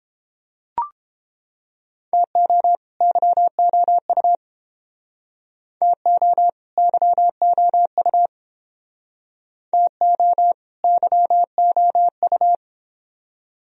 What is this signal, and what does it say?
Morse code, heard as a keyed tone, about 30 wpm, 700 Hz. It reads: TO YOU TO YOU TO YOU